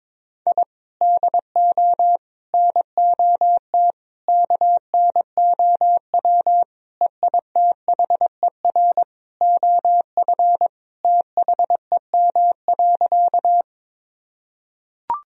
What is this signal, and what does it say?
Morse code, heard as a keyed tone, about 22 wpm, 700 Hz. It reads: I DO NOT KNOW EITHER OF THEM.
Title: Sentences from Top 1000 Words - Review Part 4 - 22wpm